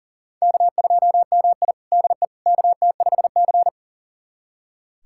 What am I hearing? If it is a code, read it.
Morse code, heard as a keyed tone, about 40 wpm, 700 Hz. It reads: K2MI DE KT5C
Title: USA Callsign DE Callsign - 4 Characters - 40wpm